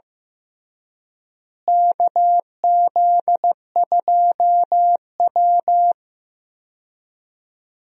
Morse code, heard as a keyed tone, about 15 wpm, 700 Hz. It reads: KZ2W
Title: USA Call Signs - 4 Characters - Repeated 3x - 15wpm